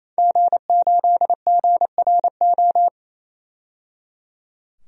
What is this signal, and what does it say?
Morse code, heard as a keyed tone, about 27 wpm, 700 Hz. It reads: G8GRO
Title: DXCC-100 Call Signs - 5 Characters - 28wpm